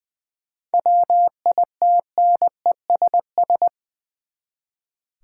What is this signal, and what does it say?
Morse code, heard as a keyed tone, about 20 wpm, 700 Hz. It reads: WITNESS
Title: Seven Letter Words - 20wpm